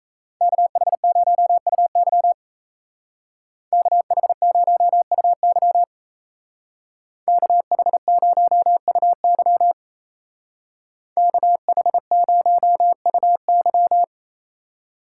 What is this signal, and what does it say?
Morse code, heard as a keyed tone, about 34 wpm, 700 Hz. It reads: KH0UY KH0UY KH0UY KH0UY
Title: Speed Racing - USA 2x2 Call Signs 28wpm